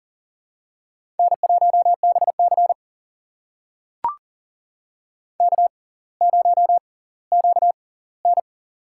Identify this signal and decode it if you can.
Morse code, heard as a keyed tone, about 38 wpm, spaced as for 20 wpm, 690 Hz. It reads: N1BC K0QN
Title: USA Call Signs - 4 Characters - 6x Character Spacing - 40wpm